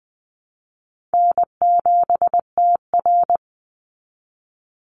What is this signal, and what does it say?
Morse code, heard as a keyed tone, about 20 wpm, 700 Hz. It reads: N7TR
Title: Call Signs Repeated 2x - 20wpm